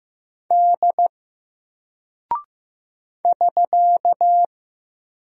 Morse code, heard as a keyed tone, about 15 wpm, 700 Hz. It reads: D <SK>
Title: Single Character (Letter:Number:Punctuation:Prosign) 15wpm